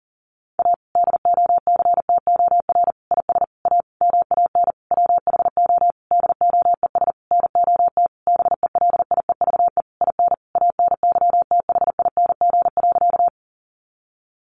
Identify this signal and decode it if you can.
Morse code, heard as a keyed tone, about 40 wpm, 700 Hz. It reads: A DOCTOR IS A MAN WHO DOES NOT BELIEVE IN ANYTHING.
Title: Sentences from Top 1500 Words 40wpm